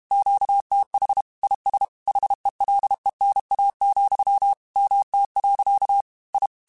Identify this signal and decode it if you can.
Morse code, heard as a keyed tone, about 32 wpm, 800 Hz. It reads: QTH IS HELENA, MT. I
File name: QSOs / QSOs-mp3s / WD0BGP 1@30